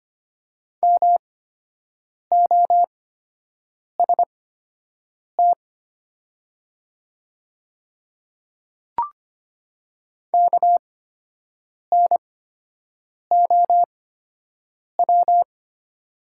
Morse code, heard as a keyed tone, about 25 wpm, spaced as for 7 wpm, 700 Hz. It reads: MOST KNOW
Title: 8x Character Spacing - Top 100 Words - 25wpm